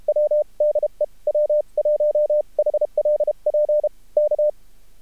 Morse code, heard as a keyed tone, about 34 wpm, spaced as for 29 wpm, 600 Hz. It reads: W DE W1HLP K
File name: archived / CRSnet